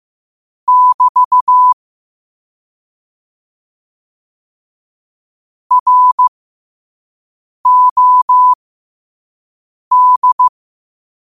Morse code, heard as a keyed tone, about 15 wpm, spaced as for 5 wpm, 1000 Hz. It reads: <BT> ROD
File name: qotdfiles / 5wpm